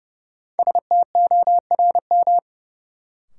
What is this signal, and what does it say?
Morse code, heard as a keyed tone, about 31 wpm, 700 Hz. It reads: STORM
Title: Sentences from Top 1500 Words - New Word Drill Alt - 30wpm